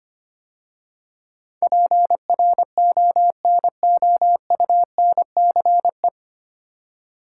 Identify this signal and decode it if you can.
Morse code, heard as a keyed tone, about 25 wpm, 700 Hz. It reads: PRONOUNCE
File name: practice / 25wpm / 9 Letter Words - First 3 Letters Given - 25wpm